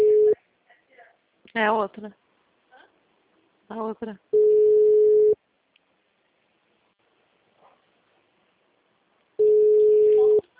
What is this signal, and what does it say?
Ringback tone (Europe, 425 Hz), with speech.
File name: out- -2036- - - .38708